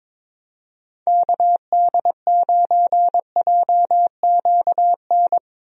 Morse code, heard as a keyed tone, about 22 wpm, 700 Hz. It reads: KD9JQN
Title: USA Call Signs - 6 Characters - 22wpm